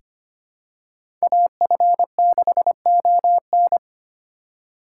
Morse code, heard as a keyed tone, about 25 wpm, 700 Hz. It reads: AF6ON
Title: USA Call Signs - 5 Characters - 2x Character Spacing - 25wpm